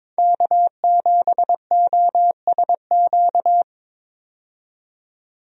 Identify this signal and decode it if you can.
Morse code, heard as a keyed tone, about 22 wpm, 700 Hz. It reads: K7OSQ
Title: USA Call Signs - 5 Characters - 22wpm